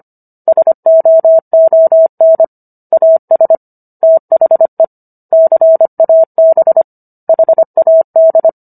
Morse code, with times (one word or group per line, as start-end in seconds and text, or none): 0.5-2.4 SOON
2.9-3.5 AS
4.0-4.8 THE
5.3-6.8 CAB
7.3-8.5 HAD